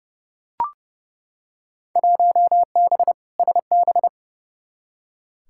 Morse code, heard as a keyed tone, about 32 wpm, 700 Hz. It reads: 1B SB
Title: Field Day Exchanges - 30wpm